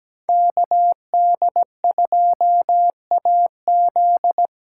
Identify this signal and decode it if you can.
Morse code, heard as a keyed tone, about 17 wpm, 700 Hz. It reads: KD2AZ
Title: USA Call Signs - 8x Character Spacing - 17wpm